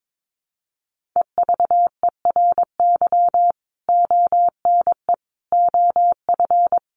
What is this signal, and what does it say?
Morse code, heard as a keyed tone, about 22 wpm, 700 Hz. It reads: EVERY ONE OF
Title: Sentences from Top 1500 Words 22wpm